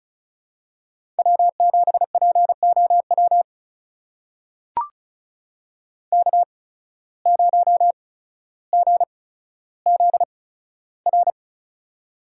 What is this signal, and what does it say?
Morse code, heard as a keyed tone, about 36 wpm, spaced as for 15 wpm, 700 Hz. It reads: W7POW K0GZR
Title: USA Call Signs - 5 Characters - 8x Character Spacing - 35wpm